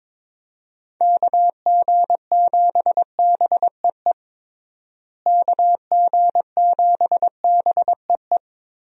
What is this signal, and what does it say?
Morse code, heard as a keyed tone, about 22 wpm, 700 Hz. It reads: KG7BEE KG7BEE
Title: USA Call Signs - 6 Characters - Repeated 3x - 22wpm